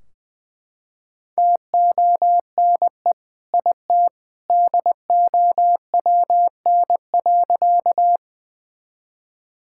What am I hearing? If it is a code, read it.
Morse code, heard as a keyed tone, about 21 wpm, 700 Hz. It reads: TONE IT DOWN.
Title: Sentences from Top 1600 Words 20wpm